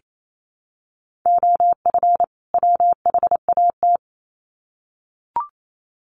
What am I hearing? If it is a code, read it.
Morse code, heard as a keyed tone, about 28 wpm, 700 Hz. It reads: OF WHAT
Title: Sets of 2 Words 28wpm